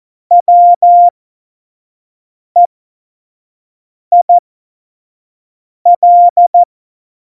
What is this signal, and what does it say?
Morse code, heard as a keyed tone, about 14 wpm, spaced as for 5 wpm, 700 Hz. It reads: WEIL
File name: Morsekurs / Witze / Witz 06 March 2018, 25 F70